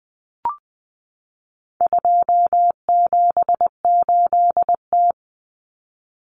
Morse code, heard as a keyed tone, about 20 wpm, 700 Hz. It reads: 278T
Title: Common Cut Numbers - 4-Digit 20wpm